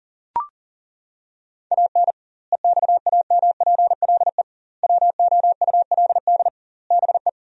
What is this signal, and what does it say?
Morse code, heard as a keyed tone, about 40 wpm, 700 Hz. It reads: AN EXAMPLE WOULD BE
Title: 2x Word-Spacing - Sentences from Top 400 Words - 40wpm